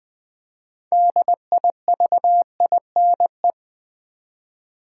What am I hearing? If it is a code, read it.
Morse code, heard as a keyed tone, about 20 wpm, 700 Hz. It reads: DIVINE
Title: Build A Word - 2 Letters At A Time - 6-Letter Words - 20wpm